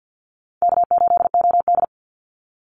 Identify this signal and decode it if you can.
Morse code, heard as a keyed tone, about 50 wpm, 700 Hz. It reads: K8OD
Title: Easy Call Signs - 50wpm